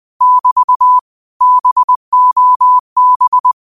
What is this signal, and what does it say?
Morse code, heard as a keyed tone, about 20 wpm, 1000 Hz. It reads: <BT> BOB
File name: qotdfiles / 20wpm